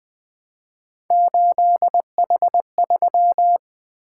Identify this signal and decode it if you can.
Morse code, heard as a keyed tone, about 23 wpm, 700 Hz. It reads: 8H3
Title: Speed-Racing - Three Letter-Number Combinations - 20wpm